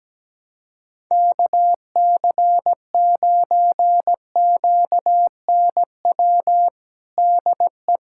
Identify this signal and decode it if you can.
Morse code, heard as a keyed tone, about 17 wpm, 700 Hz. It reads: KC9QNW DE
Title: USA Callsign DE Callsign 17wpm